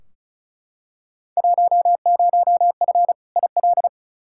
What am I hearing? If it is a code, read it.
Morse code, heard as a keyed tone, about 36 wpm, 700 Hz. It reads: 10F IL
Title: Field Day Exchanges Encore - 35wpm